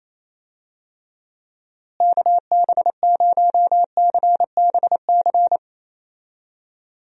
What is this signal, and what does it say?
Morse code, heard as a keyed tone, about 28 wpm, 700 Hz. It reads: KB0CBC
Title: USA Call Signs - 6 Characters - Repeated 2x - 28wpm